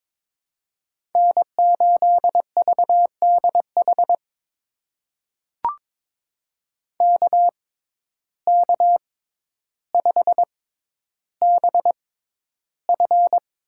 Morse code, heard as a keyed tone, about 22 wpm, spaced as for 11 wpm, 700 Hz. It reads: N8VDH KK5BF
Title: USA Call Signs - 6x Character Spacing - 22wpm